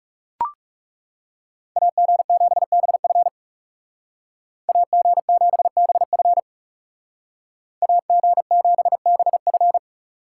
Morse code, heard as a keyed tone, about 40 wpm, 700 Hz. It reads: AG7BF AG7BF AG7BF